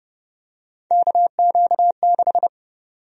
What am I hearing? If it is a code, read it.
Morse code, heard as a keyed tone, about 30 wpm, 700 Hz. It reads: KQ6
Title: USA Call Sign Prefixes - 30wpm